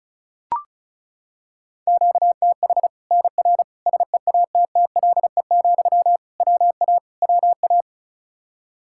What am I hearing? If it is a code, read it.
Morse code, heard as a keyed tone, about 34 wpm, 700 Hz. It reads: QTH NR SEATTLE, WA WA
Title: Practice QSO - Exchange 1 - Beginning - 35wpm